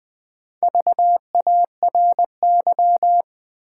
Morse code, heard as a keyed tone, about 20 wpm, 700 Hz. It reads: VARY
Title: Sentences from Top 1200 Words - New Word Drill Alt - 20wpm